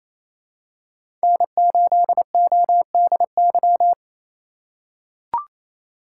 Morse code, heard as a keyed tone, about 28 wpm, 700 Hz. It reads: N8ODY
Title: USA Call Signs - 5 Characters - 2x Character Spacing - 28wpm